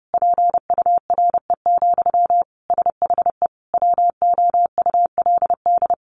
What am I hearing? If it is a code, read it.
Morse code, heard as a keyed tone, about 30 wpm, 700 Hz. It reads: PURE, SHE WOULD
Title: Sentences from Top 1500 Words 30wpm